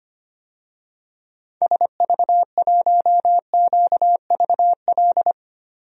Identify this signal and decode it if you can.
Morse code, heard as a keyed tone, about 25 wpm, 700 Hz. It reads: SV1QVL
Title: DXCC-100 Call Signs - 6 Characters - Repeated 2x - 25wpm